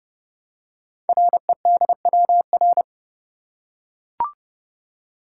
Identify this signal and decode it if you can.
Morse code, heard as a keyed tone, about 30 wpm, 690 Hz. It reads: REDWR
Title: Code Groups - 5 Letters - 30wpm